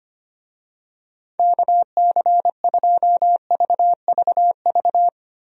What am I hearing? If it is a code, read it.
Morse code, heard as a keyed tone, about 25 wpm, 700 Hz. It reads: KC2VVV
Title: USA Call Signs - 6 Characters - Repeated 2x - 25wpm